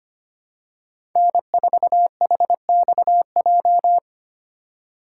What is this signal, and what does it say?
Morse code, heard as a keyed tone, about 25 wpm, 700 Hz. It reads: N4HXJ
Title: USA Call Signs - 2x Character Spacing - 25wpm